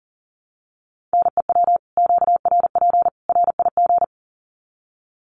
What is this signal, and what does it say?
Morse code, heard as a keyed tone, about 40 wpm, 700 Hz. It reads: NEW QRP RIG
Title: Common QSO Phrases 40wpm